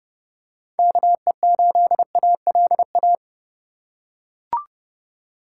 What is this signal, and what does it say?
Morse code, heard as a keyed tone, about 30 wpm, 700 Hz. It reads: KE8ALA